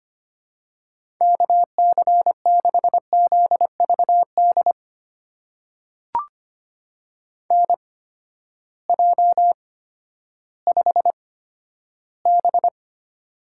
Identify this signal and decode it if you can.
Morse code, heard as a keyed tone, about 25 wpm, spaced as for 13 wpm, 700 Hz. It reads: KC6ZVD NJ5B
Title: USA Call Signs - 8x Character Spacing - 25wpm